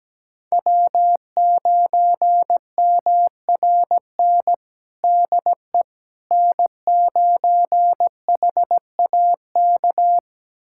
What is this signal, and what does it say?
Morse code, heard as a keyed tone, about 17 wpm, 700 Hz. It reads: W9MRN DE N9HAK
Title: USA Callsign DE Callsign - 5 Characters - 17wpm